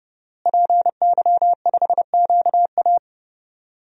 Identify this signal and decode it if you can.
Morse code, heard as a keyed tone, about 31 wpm, 700 Hz. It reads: PY5QA
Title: DXCC-100 Call Signs - 5 Characters - Repeated 3x - 30wpm